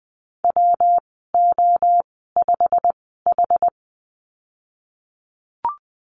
Morse code, heard as a keyed tone, about 20 wpm, spaced as for 15 wpm, 700 Hz. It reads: WO5H
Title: 122 Lesson 20g - Number 3 - Call Signs 20wpm